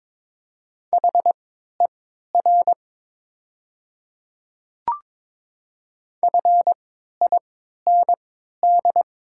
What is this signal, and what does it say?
Morse code, heard as a keyed tone, about 22 wpm, spaced as for 13 wpm, 700 Hz. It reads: HER FIND
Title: 3x Character Spacing - Top 100 Words - 22wpm